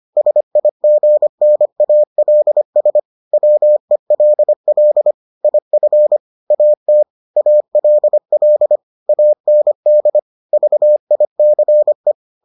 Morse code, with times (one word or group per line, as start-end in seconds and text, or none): 0.2-3.0 SIGNALS
3.3-5.1 WELL
5.4-6.2 IF
6.5-7.0 AT
7.4-8.8 ALL
9.1-10.2 AND
10.5-12.1 VICE